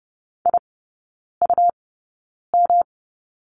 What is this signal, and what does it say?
Morse code, heard as a keyed tone, about 30 wpm, spaced as for 9 wpm, 700 Hz. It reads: IUM
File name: Lesson 18e - Letter m - Three Characters 30wpm